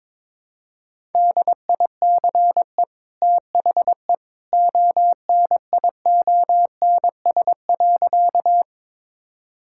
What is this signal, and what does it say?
Morse code, heard as a keyed tone, about 22 wpm, 700 Hz. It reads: DICE THE ONIONS.
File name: practice / 22wpm / Easy Sentences with Hints - 22wpm